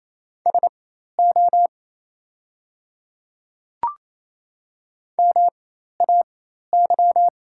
Morse code, heard as a keyed tone, about 28 wpm, spaced as for 13 wpm, 700 Hz. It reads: SO MAY
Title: 4x Character Spacing - Top 100 Words - 28wpm